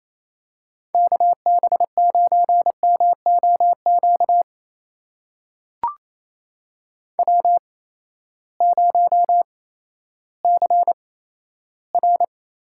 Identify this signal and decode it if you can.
Morse code, heard as a keyed tone, about 28 wpm, spaced as for 14 wpm, 700 Hz. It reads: KB9MOQ W0CR
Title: USA Call Signs - 8x Character Spacing - 28wpm